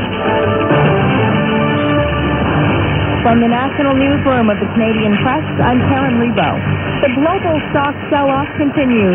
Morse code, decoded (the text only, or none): none